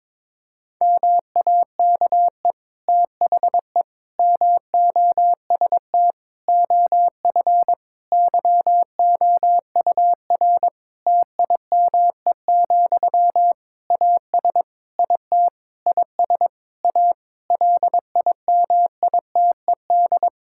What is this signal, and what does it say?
Morse code, heard as a keyed tone, about 22 wpm, 700 Hz. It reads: MAKE THE MOST OF YOUR TIME, AS IT IS A LIMITED
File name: Difficult Sentences with Hints - 22wpm